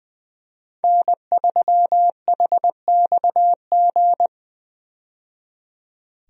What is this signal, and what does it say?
Morse code, heard as a keyed tone, about 21 wpm, 700 Hz. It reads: N3HXG
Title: USA Call Signs - 5 Characters - 20wpm